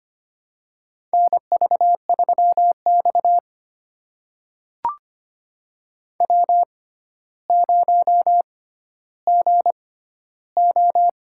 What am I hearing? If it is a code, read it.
Morse code, heard as a keyed tone, about 25 wpm, spaced as for 13 wpm, 700 Hz. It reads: NV3X W0GO
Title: USA Call Signs - 4 Characters - 6x Character Spacing - 25wpm